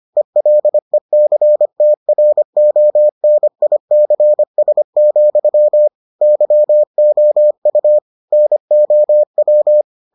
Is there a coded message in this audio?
Yes: Morse code from 0.2 to 9.8 s, heard as a keyed tone, about 25 wpm, 600 Hz. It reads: ELECTRONICS, YOU NOW